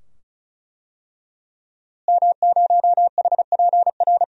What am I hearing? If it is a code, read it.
Morse code, heard as a keyed tone, about 34 wpm, 700 Hz. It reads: M0HPR
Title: DXCC-100 Call Signs - 5 Characters - Repeated 2x - 35wpm